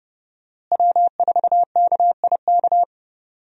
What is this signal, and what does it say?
Morse code, heard as a keyed tone, about 32 wpm, 700 Hz. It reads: W4KIK
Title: USA Call Signs - 5 Characters - Repeated 2x - 30wpm